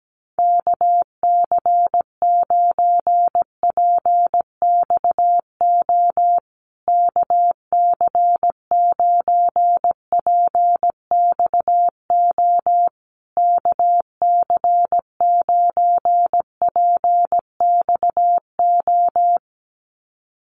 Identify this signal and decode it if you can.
Morse code, heard as a keyed tone, about 17 wpm, 700 Hz. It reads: KC9PXO KC9PXO KC9PXO
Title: USA Call Signs Repeated 3x 17wpm